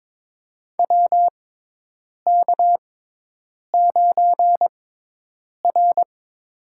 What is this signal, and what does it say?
Morse code, heard as a keyed tone, about 21 wpm, spaced as for 8 wpm, 700 Hz. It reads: WK9R
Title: USA Call Signs - 6x Character Spacing - 22wpm